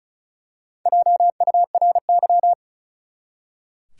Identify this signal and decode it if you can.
Morse code, heard as a keyed tone, about 36 wpm, 700 Hz. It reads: JURY
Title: Four Letter Words - 35wpm